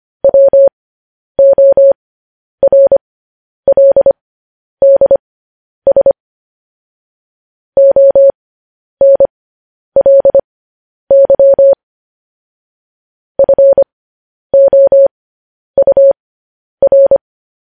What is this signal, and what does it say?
Morse code, heard as a keyed tone, about 25 wpm, spaced as for 10 wpm, 550 Hz. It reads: WORLDS ONLY FOUR